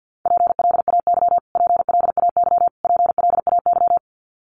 Morse code, heard as a keyed tone, about 50 wpm, 700 Hz. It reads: PLAY PLAY PLAY
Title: Top 500 Words Repeated 50wpm